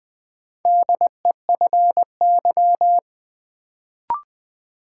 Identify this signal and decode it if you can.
Morse code, heard as a keyed tone, about 20 wpm, 700 Hz. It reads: DEFY
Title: Build A Word - 1 Letter At A Time - 3 to 5-Letter Words - 20wpm